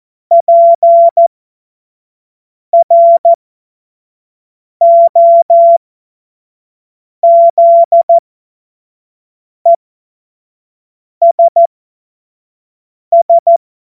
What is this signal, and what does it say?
Morse code, heard as a keyed tone, about 14 wpm, spaced as for 5 wpm, 680 Hz. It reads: PROZESS
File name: Morsekurs / Witze / Witz 21. Juni 2022 25 F70